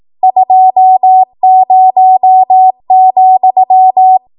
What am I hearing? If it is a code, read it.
Morse code, heard as a keyed tone, about 18 wpm, 750 Hz. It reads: 20,